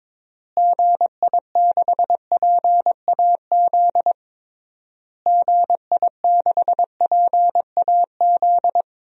Morse code, heard as a keyed tone, about 21 wpm, 700 Hz. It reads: GI6PAZ GI6PAZ
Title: DXCC-100 Call Signs - 6 Characters - Repeated 3x - 22wpm